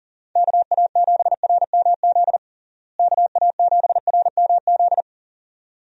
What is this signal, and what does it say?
Morse code, heard as a keyed tone, about 37 wpm, 700 Hz. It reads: KA7RMZ KA7RMZ
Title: USA Call Signs - 6 Characters - Repeated 2x - 40wpm